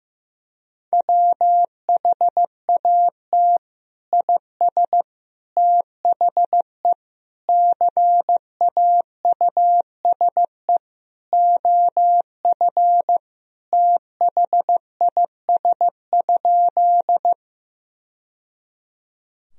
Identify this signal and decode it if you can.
Morse code, heard as a keyed tone, about 15 wpm, 700 Hz. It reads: WHAT IS THE CAUSE OF THIS?
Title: Sentences from Top 1000 Words - Review Part 1 - 15wpm